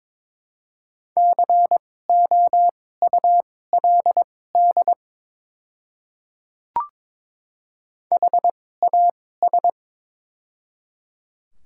Morse code, heard as a keyed tone, about 22 wpm, spaced as for 16 wpm, 700 Hz. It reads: COULD HAS
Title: 2x Character Spacing - Top 100 Words - 22wpm